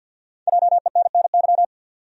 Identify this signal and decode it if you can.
Morse code, heard as a keyed tone, about 46 wpm, 700 Hz. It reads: JENNY
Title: USA Common Names - Max 5 Letters - 50wpm